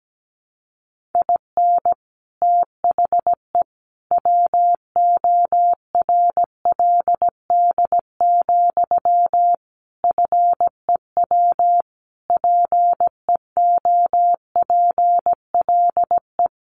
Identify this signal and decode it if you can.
Morse code, heard as a keyed tone, about 17 wpm, 700 Hz. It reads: IN THE WORLD, FEW PEOPLE
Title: Sentences from Top 1400 Words 17wpm